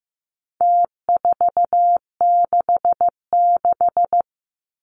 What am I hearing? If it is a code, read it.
Morse code, heard as a keyed tone, about 15 wpm, 700 Hz. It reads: T466
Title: Numbers - Contest Style - 4-Digit 15wpm